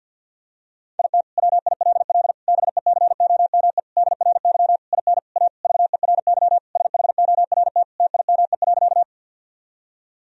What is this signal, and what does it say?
Morse code, heard as a keyed tone, about 50 wpm, 700 Hz. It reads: IT WILL BECOME DRY IN A VERY SHORT TIME.
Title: Sentences from Top 1000 Words - Review Part 3 - 50wpm